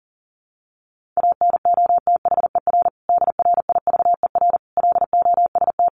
Morse code, heard as a keyed tone, about 37 wpm, 700 Hz. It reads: ANOTHER DRIVER LOST
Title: Sentences from Top 1400 Words 40wpm